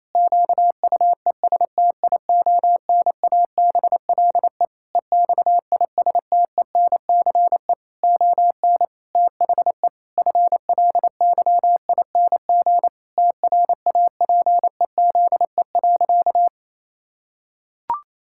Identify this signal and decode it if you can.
Morse code, heard as a keyed tone, about 28 wpm, 700 Hz. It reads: QUESTIONABLE EXISTENCE ON THE FLYING TRAPEZE.
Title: Pangram Challenge v2 - 28wpm